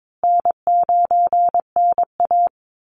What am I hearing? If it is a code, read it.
Morse code, heard as a keyed tone, about 22 wpm, 690 Hz. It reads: N9NA
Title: Easy Call Signs Repeated 2x - 22wpm